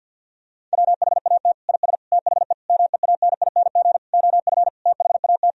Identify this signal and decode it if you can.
Morse code, heard as a keyed tone, about 50 wpm, 700 Hz. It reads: WHAT IS THE MEANING OF THAT
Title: Sentences from Top 1000 Words - Review Part 4 - 50wpm